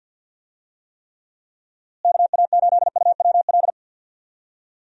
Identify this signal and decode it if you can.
Morse code, heard as a keyed tone, about 45 wpm, 700 Hz. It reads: KA8UWL
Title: USA Call Signs - 6 Characters - Repeated 3x - 50wpm